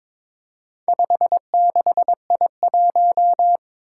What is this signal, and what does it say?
Morse code, heard as a keyed tone, about 22 wpm, 700 Hz. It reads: 56I1
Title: Four Challenging Letter-Number Combinations 22wpm